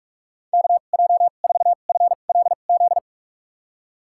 Morse code, heard as a keyed tone, about 42 wpm, spaced as for 31 wpm, 700 Hz. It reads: KJ4FLZ
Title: USA Call Signs - 6 Characters - 2x Character Spacing - 45wpm